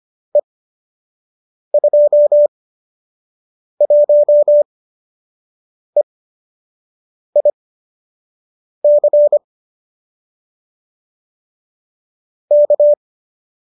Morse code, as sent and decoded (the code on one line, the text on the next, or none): . ..--- .---- . .. -.-. / -.-
E21EIC K